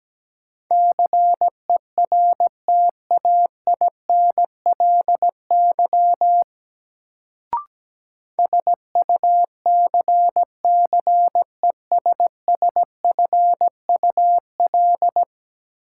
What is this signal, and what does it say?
Morse code, heard as a keyed tone, about 17 wpm, 700 Hz. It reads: CERTAINLY SUCCESSFUL
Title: Sentences from Top 900 Words - New Word Drill - 17wpm